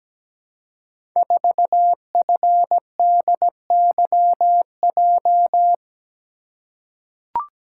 Morse code, heard as a keyed tone, about 17 wpm, 700 Hz. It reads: 4FDYJ
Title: Code Groups - 5 Letter:Numbers - 17wpm